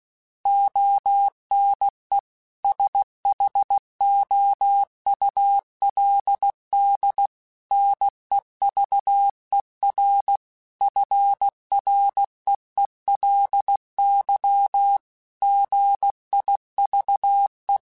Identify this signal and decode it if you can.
Morse code, heard as a keyed tone, about 16 wpm, 800 Hz. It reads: ONE SHOULD NEVER FREELY GIVE
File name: Quotes / Quotes-mp3s / Advice @15